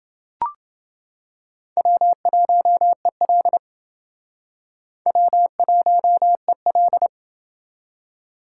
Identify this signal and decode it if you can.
Morse code, heard as a keyed tone, about 28 wpm, 700 Hz. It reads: W1EL W1EL